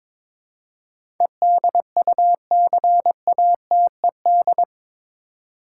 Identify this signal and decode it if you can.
Morse code, heard as a keyed tone, about 22 wpm, 700 Hz. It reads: EDUCATED
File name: practice / 22wpm / Eight Letter Words - 22wpm